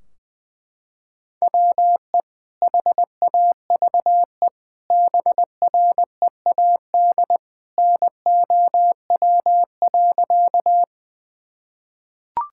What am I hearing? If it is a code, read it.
Morse code, heard as a keyed tone, about 21 wpm, 700 Hz. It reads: WE HAVE BREAD NOW.
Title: Sentences from Top 1600 Words 20wpm